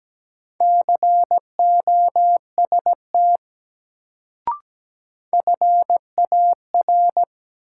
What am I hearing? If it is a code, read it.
Morse code, heard as a keyed tone, about 17 wpm, 700 Hz. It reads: COST FAR
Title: Sets of 3 Words - New Word Drill - 17wpm